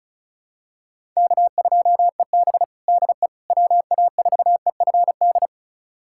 Morse code, heard as a keyed tone, about 34 wpm, 700 Hz. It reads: K2EB DE WA4EFD
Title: USA Callsign DE Callsign 35wpm